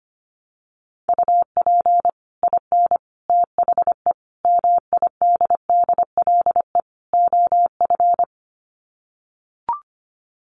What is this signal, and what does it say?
Morse code, heard as a keyed tone, about 25 wpm, 700 Hz. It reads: UP IN THE MIDDLE OF
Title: Sets of 5 Words Encore 25wpm